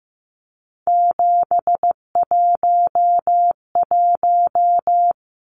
Morse code, heard as a keyed tone, about 15 wpm, 700 Hz. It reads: 711
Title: Numbers - 3-Digit 15wpm